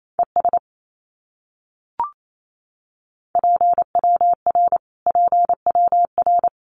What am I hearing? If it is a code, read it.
Morse code, heard as a keyed tone, about 28 wpm, 700 Hz. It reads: ES PWR PWR
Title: Abbreviations Repeated 3x - 28wpm